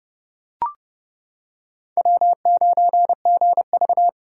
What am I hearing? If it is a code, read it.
Morse code, heard as a keyed tone, about 28 wpm, 700 Hz. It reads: W9GV